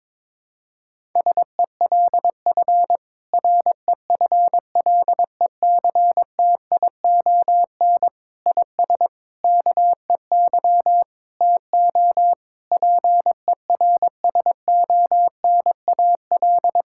Morse code, heard as a keyed tone, about 22 wpm, 700 Hz. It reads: SELF REFLECTION IS KEY TO PERSONAL
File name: Difficult Sentences with Hints - 22wpm